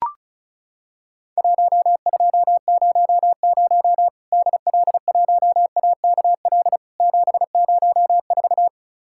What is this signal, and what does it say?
Morse code, heard as a keyed tone, about 34 wpm, 700 Hz. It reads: 1200 DL1AKL 704
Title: Worked All Europe DX Contest - QTC Exchange - 35wpm